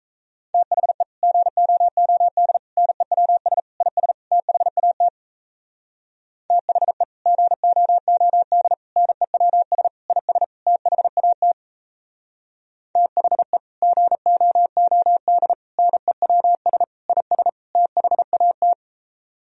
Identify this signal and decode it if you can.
Morse code, heard as a keyed tone, about 37 wpm, 700 Hz. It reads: THE GOOD NEWS IS THAT THE GOOD NEWS IS THAT THE GOOD NEWS IS THAT